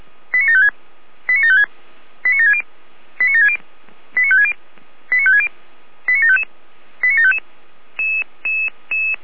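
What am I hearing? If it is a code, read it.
Five-tone selective call (ZVEI), heard as quick runs of five tones, tones of about 70 ms. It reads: 89756 89756 89760 89760 89560 89560 89755 89755
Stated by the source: ZVEI five-tone selective call received by SDR.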